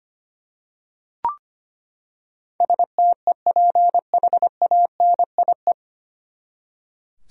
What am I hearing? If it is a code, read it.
Morse code, heard as a keyed tone, about 25 wpm, 700 Hz. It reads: STEPHANIE
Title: USA Common Names - 25wpm